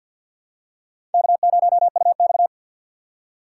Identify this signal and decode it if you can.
Morse code, heard as a keyed tone, about 50 wpm, 700 Hz. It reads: K0UX
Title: USA Call Signs - 4 Characters - 2x Character Spacing - 50wpm